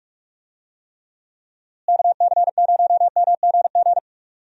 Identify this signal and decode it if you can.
Morse code, heard as a keyed tone, about 42 wpm, 700 Hz. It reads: KC0MGG
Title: USA Call Signs - 6 Characters - Repeated 2x - 45wpm